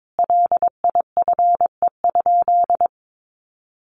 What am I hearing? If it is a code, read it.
Morse code, heard as a keyed tone, about 22 wpm, 700 Hz. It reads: LIFE?
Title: Sentences from Top 900 Words 22wpm